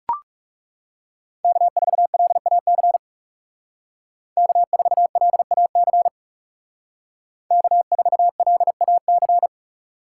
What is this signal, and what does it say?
Morse code, heard as a keyed tone, about 39 wpm, 700 Hz. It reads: K4LAC K4LAC K4LAC